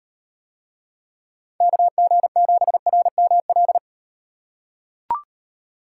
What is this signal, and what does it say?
Morse code, heard as a keyed tone, about 38 wpm, 700 Hz. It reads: KG7RML